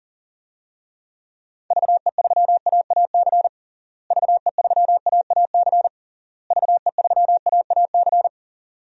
Morse code, heard as a keyed tone, about 40 wpm, 700 Hz. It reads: VE3AAC VE3AAC VE3AAC
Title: DXCC-100 Call Signs - 6 Characters - Repeated 3x - 40wpm